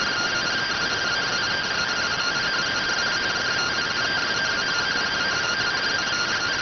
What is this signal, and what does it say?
Two-tone teleprinter signal, about 170 Hz shift.